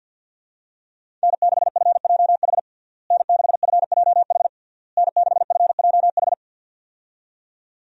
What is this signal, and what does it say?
Morse code, heard as a keyed tone, about 52 wpm, 700 Hz. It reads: N6FJH N6FJH N6FJH
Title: USA Call Signs - 5 Characters - Repeated 3x - 50wpm